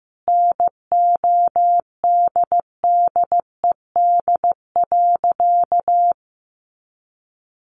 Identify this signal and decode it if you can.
Morse code, heard as a keyed tone, about 15 wpm, 700 Hz. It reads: NODDED.
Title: A Sound Of Thunder 15wpm